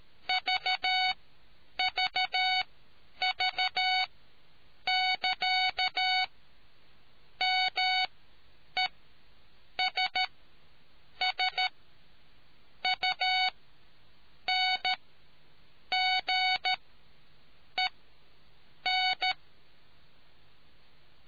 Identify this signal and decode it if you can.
Morse code, heard as a keyed tone, about 13 wpm, spaced as for 7 wpm, 740 Hz. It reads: VVV<KA>MESSUNGEN